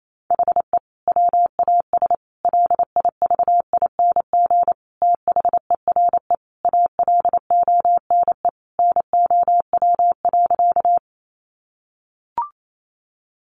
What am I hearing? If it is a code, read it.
Morse code, heard as a keyed tone, about 28 wpm, 700 Hz. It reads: HE WAS LIVING THERE ALONE NOW.
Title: Sentences from Top 1100 Words 28wpm